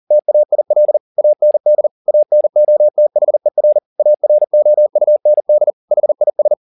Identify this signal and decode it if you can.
Morse code, heard as a keyed tone, about 40 wpm, 600 Hz. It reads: TAIL AND ANOTHER AROUND HIS
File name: the old man and the sea 2 40wpm